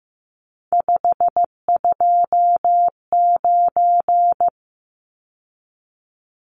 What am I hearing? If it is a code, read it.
Morse code, heard as a keyed tone, about 15 wpm, 700 Hz. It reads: 529
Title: Numbers - 3-Digit 15wpm